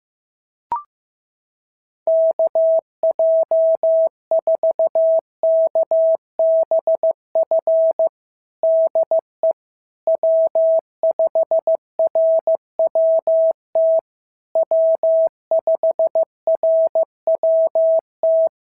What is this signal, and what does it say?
Morse code, heard as a keyed tone, about 15 wpm, 650 Hz. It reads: KJ4KBF DE W5RWT W5RWT
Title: SOTA QSO Practice 15wpm